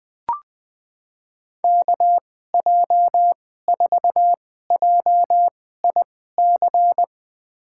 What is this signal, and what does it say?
Morse code, heard as a keyed tone, about 20 wpm, spaced as for 15 wpm, 700 Hz. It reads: KJ4JIC
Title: USA Call Signs - 6 Characters - 2x Character Spacing - 20wpm